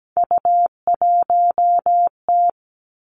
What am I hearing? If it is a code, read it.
Morse code, heard as a keyed tone, about 17 wpm, 700 Hz. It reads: U1T
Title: Three Challenging Letter-Number Combinations 17wpm